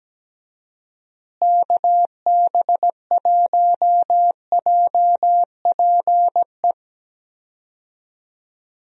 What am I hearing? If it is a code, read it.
Morse code, heard as a keyed tone, about 17 wpm, 700 Hz. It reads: KB1JPE
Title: USA Call Signs - 6 Characters - Repeated 3x - 17wpm